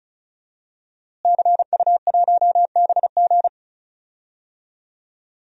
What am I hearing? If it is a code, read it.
Morse code, heard as a keyed tone, about 34 wpm, 700 Hz. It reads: CU1BG
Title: DXCC-100 Call Signs - 5 Characters - Repeated 2x - 35wpm